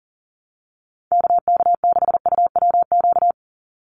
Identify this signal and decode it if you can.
Morse code, heard as a keyed tone, about 40 wpm, 700 Hz. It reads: KK6UWQ
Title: USA Call Signs 40wpm